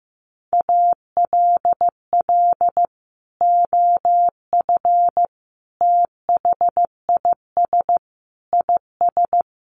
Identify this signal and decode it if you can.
Morse code, heard as a keyed tone, about 15 wpm, 700 Hz. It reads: ALL OF THIS IS
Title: Sentences from Top 1100 Words 15wpm